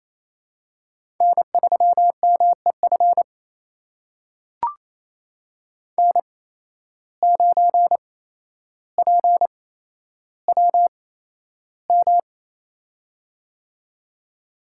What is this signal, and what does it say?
Morse code, heard as a keyed tone, about 28 wpm, spaced as for 12 wpm, 700 Hz. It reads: N3MEF N9PWM
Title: USA Call Signs - 5 Characters - 8x Character Spacing - 28wpm